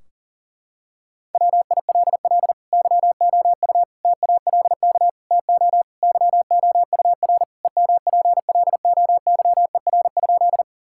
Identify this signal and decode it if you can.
Morse code, heard as a keyed tone, about 40 wpm, 700 Hz. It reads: WILL YOU TALK TO YOUR EMPLOYER?
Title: Sentences from Top 1600 Words 40wpm